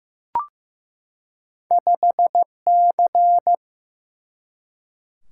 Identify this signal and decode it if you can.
Morse code, heard as a keyed tone, about 15 wpm, 700 Hz. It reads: 5C
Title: Single-Letter Single-Number 15wpm